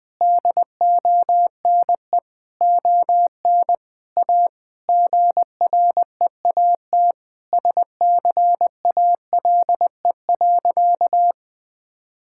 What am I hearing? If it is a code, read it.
Morse code, heard as a keyed tone, about 20 wpm, 700 Hz. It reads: DONE ON A GREAT SCALE.
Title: Sentences from Top 1000 Words - Review Part 5 - 20wpm